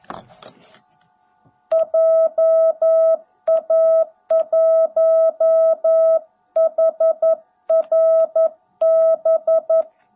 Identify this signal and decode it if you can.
Morse code, heard as a keyed tone, about 11 wpm, 660 Hz. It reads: JA1HRB
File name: jh8cbh / 2021cw / cw18b